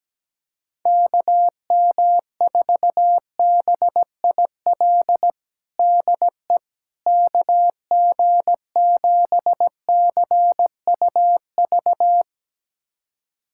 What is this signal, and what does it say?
Morse code, heard as a keyed tone, about 17 wpm, 700 Hz. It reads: KM4BIL DE KG7CUV
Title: USA Callsign DE Callsign - 6 Characters - 17wpm